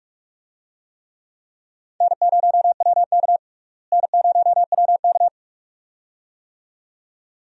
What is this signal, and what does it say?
Morse code, heard as a keyed tone, about 41 wpm, 700 Hz. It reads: N0WK N0WK
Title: USA Call Signs - 4 Characters - Repeated 2x - 45wpm